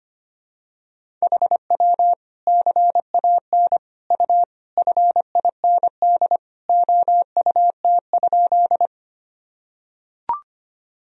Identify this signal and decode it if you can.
Morse code, heard as a keyed tone, about 25 wpm, 700 Hz. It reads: HW CAN U FIND OUT?
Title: Sentences from Top 200 Words and Abbreviations 25wpm